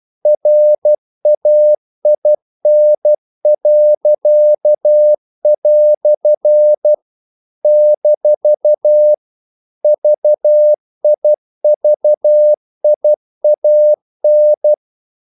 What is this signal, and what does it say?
Morse code, heard as a keyed tone, about 12 wpm, 600 Hz. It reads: RAIN." - VIVIAN
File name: CW / Practice / EnglishQuotations0112